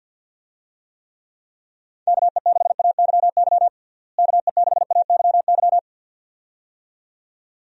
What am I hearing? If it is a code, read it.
Morse code, heard as a keyed tone, about 50 wpm, 700 Hz. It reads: KE6AYY KE6AYY
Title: USA Call Signs - 6 Characters - Repeated 2x - 50wpm